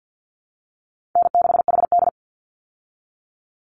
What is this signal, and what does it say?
Morse code, heard as a keyed tone, about 50 wpm, 700 Hz. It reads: N6HD
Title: Easy Call Signs Repeated 2x - 50wpm